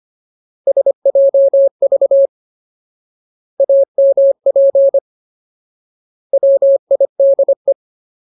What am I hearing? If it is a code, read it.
Morse code, heard as a keyed tone, about 25 wpm, 550 Hz. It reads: SJV AMP WIDE
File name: Ultimate Word List / 25WPM Ultimate